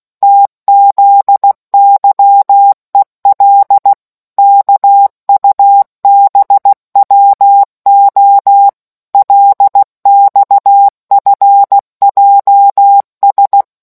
Morse code, heard as a keyed tone, about 16 wpm, 800 Hz. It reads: TZYEL KUBWO LXFJS